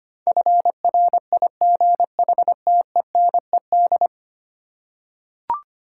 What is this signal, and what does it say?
Morse code, heard as a keyed tone, about 25 wpm, 700 Hz. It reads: FRIGHTENED
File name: Ten Letter Words - 25wpm